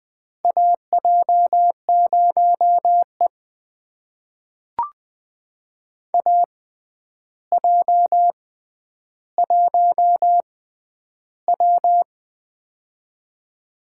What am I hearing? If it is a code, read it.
Morse code, heard as a keyed tone, about 20 wpm, spaced as for 10 wpm, 700 Hz. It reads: AJ0E AJ1W
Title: USA Call Signs - 4 Characters - 6x Character Spacing - 20wpm